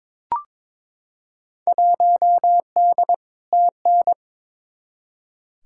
Morse code, heard as a keyed tone, about 22 wpm, 700 Hz. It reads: 1D TN
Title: Field Day Exchanges - 22wpm